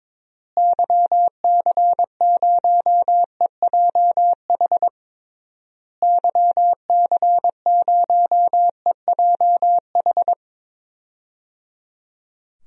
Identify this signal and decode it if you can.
Morse code, heard as a keyed tone, about 22 wpm, 700 Hz. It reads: YC0EJH YC0EJH
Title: DXCC-100 Call Signs - 6 Characters - Repeated 3x - 22wpm